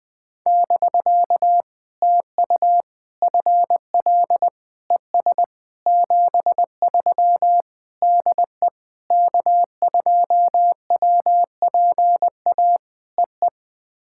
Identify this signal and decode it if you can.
Morse code, heard as a keyed tone, about 20 wpm, 700 Hz. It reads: <BK> TU FL ES 73 DE K2WPA EE
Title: POTA QSO Practice - 20wpm